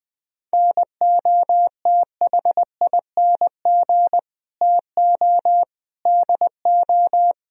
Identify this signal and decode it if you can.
Morse code, heard as a keyed tone, about 20 wpm, 700 Hz. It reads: NOTHING TO DO